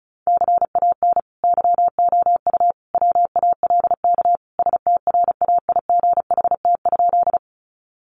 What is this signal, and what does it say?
Morse code, heard as a keyed tone, about 34 wpm, 700 Hz. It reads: CAN YOU WALK STRAIGHT?
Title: Sentences from Top 1000 Words 35wpm